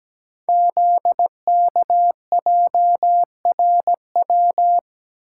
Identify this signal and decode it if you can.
Morse code, heard as a keyed tone, about 17 wpm, 700 Hz. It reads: ZKJRW
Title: Code Groups - 5 Letter:Numbers - 17wpm